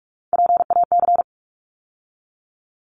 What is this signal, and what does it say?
Morse code, heard as a keyed tone, about 44 wpm, 700 Hz. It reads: PAC
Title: ARRL Sections 45wpm